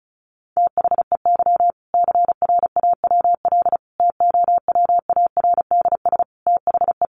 Morse code, heard as a keyed tone, about 34 wpm, 700 Hz. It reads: THEY CRAWL TOWARDS THE
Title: Easy Phrases 35wpm